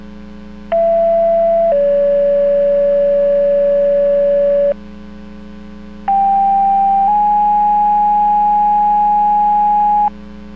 Two-tone page, heard as a one-second tone, then a three-second tone, about 670 then 570 Hz, about 790 then 830 Hz. It is heard from 0.7 to 10.1 s.